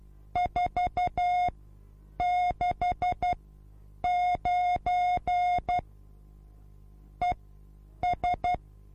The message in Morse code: 469 ES